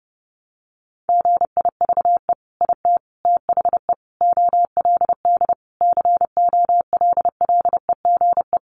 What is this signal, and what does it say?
Morse code, heard as a keyed tone, about 30 wpm, 700 Hz. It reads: GIVE IT THE OLD COLLEGE
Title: Sentences from Top 1300 Words 30wpm